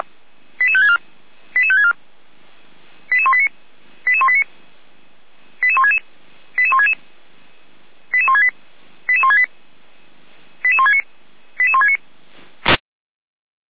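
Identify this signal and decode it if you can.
Five-tone selective call (ZVEI), heard as quick runs of five tones, tones of about 70 ms. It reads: 88564 88564 88189 88189 88166 88166 88168 88168 88169 88169
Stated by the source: ZVEI five-tone selective call received on an SDR.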